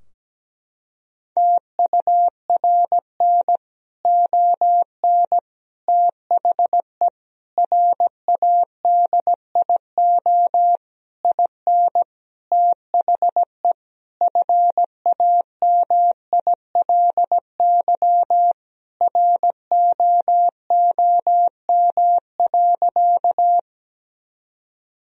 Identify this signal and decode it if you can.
Morse code, heard as a keyed tone, about 17 wpm, 700 Hz. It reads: TURN ON THE RADIO IN THE FAMILY ROOM.
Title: Sentences from Top 1000 Words - Review Part 3 - 17wpm